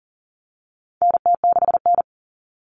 Morse code, heard as a keyed tone, about 40 wpm, 700 Hz. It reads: NT6N
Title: All Cut Numbers - 4-Digit 40wpm